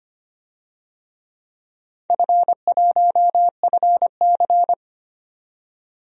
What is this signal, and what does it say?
Morse code, heard as a keyed tone, about 25 wpm, 700 Hz. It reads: F1FC
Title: DXCC-100 Call Signs - 4 Characters - 25wpm